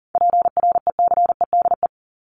Morse code, heard as a keyed tone, about 40 wpm, 700 Hz. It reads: PRECEDE
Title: ING Suffix - Top 100 Words with Base Words 40wpm